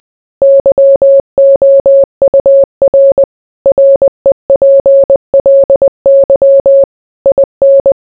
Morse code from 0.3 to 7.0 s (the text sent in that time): YOUR REPLY